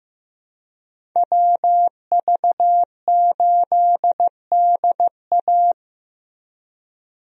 Morse code, heard as a keyed tone, about 15 wpm, 700 Hz. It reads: WV8DA
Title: USA Call Signs - 5 Characters - Repeated 3x - 15wpm